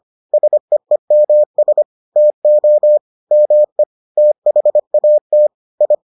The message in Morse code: SEEMS TO ME THAT I

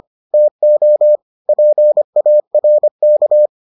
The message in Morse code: TO PARK

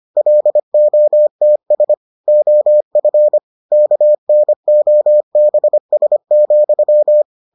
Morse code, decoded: LOTS OF KNOBS,

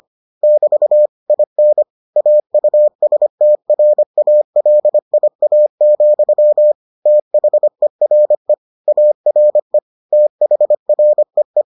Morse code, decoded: <BT> IN AUSTRALIA, THERE ARE THREE

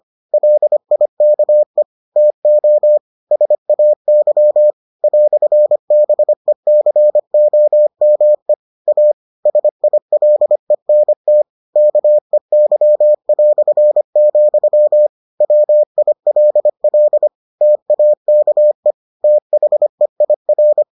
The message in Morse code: LIKE TO SAY "BECOME A SILENT KEY", WILL TAKE THEIR

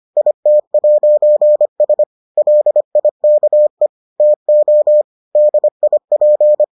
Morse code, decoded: IT'S LIKE TO DIP

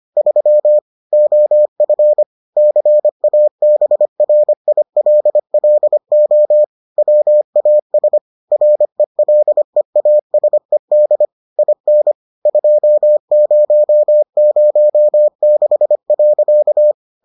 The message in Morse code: 3 OF CABRILLO WAS RELEASED IN 2006.

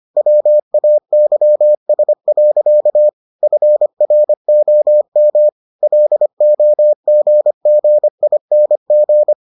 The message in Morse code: WAYS. FROM LOGGING

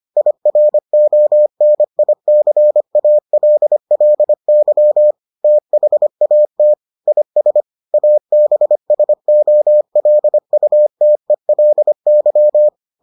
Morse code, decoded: IRONICALLY THAT IS ABSOLUTELY

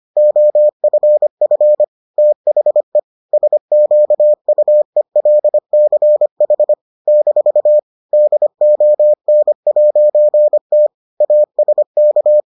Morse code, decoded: OFF THE SQUELCH - DON'T ASK

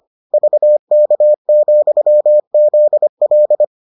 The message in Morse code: VK, ZL

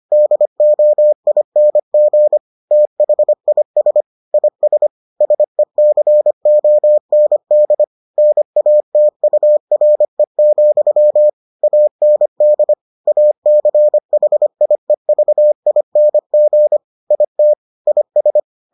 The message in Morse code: DOING THIS IS SECOND NATURE, AND ACHIEVING IT IS

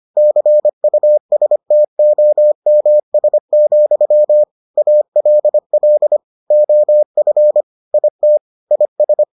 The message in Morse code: CUSTOMS, ALL OF IT IS

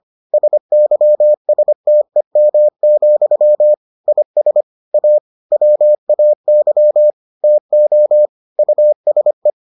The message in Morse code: SYSTEM, IS A WAY TO USE

